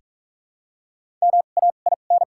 Morse code, heard as a keyed tone, about 46 wpm, spaced as for 33 wpm, 700 Hz. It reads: MAIN
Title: 2x Character Spacing - Top 500 Words - 45wpm